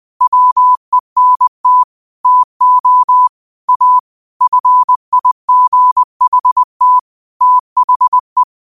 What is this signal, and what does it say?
Morse code, heard as a keyed tone, about 20 wpm, 1000 Hz. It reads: WENT TO A FIGHT THE
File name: 20wpm